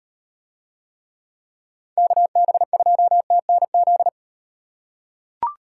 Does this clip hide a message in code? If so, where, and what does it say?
Yes: Morse code from 2.0 to 4.1 s, heard as a keyed tone, about 36 wpm, 700 Hz. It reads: KB2TNZ